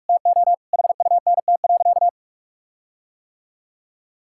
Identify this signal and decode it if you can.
Morse code, heard as a keyed tone, about 47 wpm, 700 Hz. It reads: TO HUNT.
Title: Sentences from Top 1600 Words 45wpm